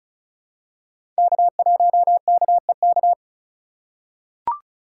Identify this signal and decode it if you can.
Morse code, heard as a keyed tone, about 36 wpm, 700 Hz. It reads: K1KEK
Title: USA Call Signs - 5 Characters - 8x Character Spacing - 35wpm